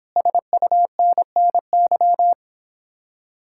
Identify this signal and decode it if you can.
Morse code, heard as a keyed tone, about 29 wpm, 700 Hz. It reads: SUNNY